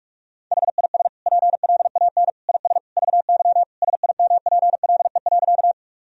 Morse code, heard as a keyed tone, about 46 wpm, 700 Hz. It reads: HIS PLAN IS VY SIMPLE.
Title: Sentences from Top 500 Words and Abbreviations 45wpm